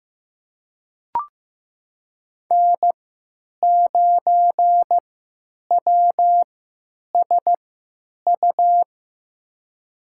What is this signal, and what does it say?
Morse code, heard as a keyed tone, about 15 wpm, spaced as for 9 wpm, 700 Hz. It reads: N9WSU
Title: USA Call Signs - 5 Characters - 3x Character Spacing - 15wpm